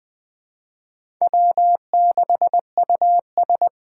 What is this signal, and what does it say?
Morse code, heard as a keyed tone, about 20 wpm, 700 Hz. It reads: W6US
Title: Speed Racing - USA Call Signs 17wpm